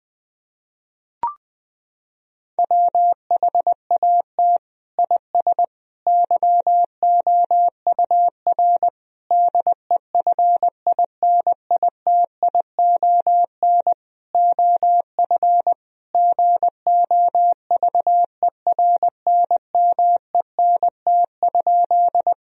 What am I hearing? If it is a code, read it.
Morse code, heard as a keyed tone, about 19 wpm, 700 Hz. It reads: WHAT IS YOUR DEFINITION OF GOVERNMENT?
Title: Sentences from Top 1000 Words - Review Part 4 - 20wpm